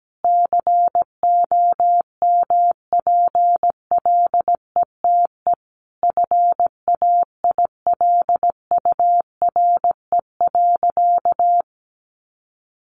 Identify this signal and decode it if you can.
Morse code, heard as a keyed tone, about 17 wpm, 700 Hz. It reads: COMPLETE FAILURE.
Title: Sentences from Top 1400 Words 17wpm